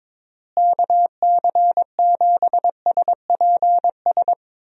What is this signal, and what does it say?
Morse code, heard as a keyed tone, about 22 wpm, 700 Hz. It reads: KC7SPS
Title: USA Call Signs - 6 Characters - Repeated 3x - 22wpm